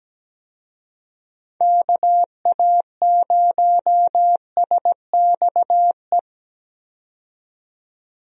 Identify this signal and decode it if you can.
Morse code, heard as a keyed tone, about 17 wpm, 700 Hz. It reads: KA0SXE
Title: USA Call Signs - 6 Characters - 17wpm